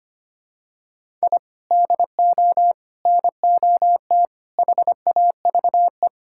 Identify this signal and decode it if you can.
Morse code, heard as a keyed tone, about 25 wpm, 700 Hz. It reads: I DO NOT HAVE
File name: practice / 25wpm / Sentences with 4-Word Phrases - 25wpm